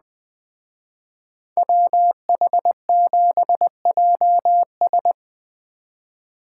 Morse code, heard as a keyed tone, about 20 wpm, 700 Hz. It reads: WH7JS